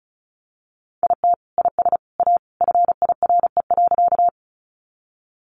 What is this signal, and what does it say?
Morse code, heard as a keyed tone, about 34 wpm, 700 Hz. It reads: IT IS A FIRE.
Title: Sentences from Top 400 Words 35wpm